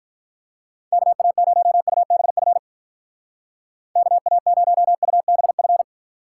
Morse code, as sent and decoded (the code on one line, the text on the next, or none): -.- .- ----- ..- -... ..-. / -.- .- ----- ..- -... ..-.
KA0UBF KA0UBF